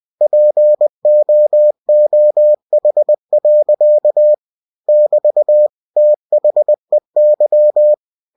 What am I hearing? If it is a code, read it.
Morse code, heard as a keyed tone, about 20 wpm, 600 Hz. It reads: POOH. <BT> THEY
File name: winnie-20wpm